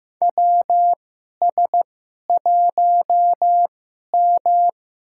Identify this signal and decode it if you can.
Morse code, heard as a keyed tone, about 15 wpm, spaced as for 11 wpm, 700 Hz. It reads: WS1M
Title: USA Call Signs - 4 Characters - 2x Character Spacing - 15wpm